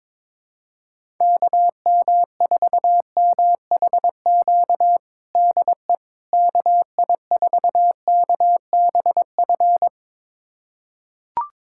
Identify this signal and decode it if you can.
Morse code, heard as a keyed tone, about 22 wpm, 700 Hz. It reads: KM4MHQ DE KI4KBF
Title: USA Callsign DE Callsign - 6 Characters - 22wpm